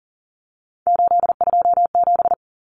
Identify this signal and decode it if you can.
Morse code, heard as a keyed tone, about 40 wpm, 700 Hz. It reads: 827
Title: Common Cut Numbers - 3-Digit 40wpm